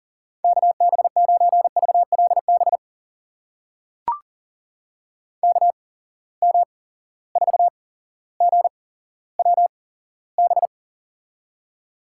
Morse code, heard as a keyed tone, about 40 wpm, spaced as for 17 wpm, 700 Hz. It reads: KB9VLB KM4GWB
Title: USA Call Signs - 6 Characters - 8x Character Spacing - 40wpm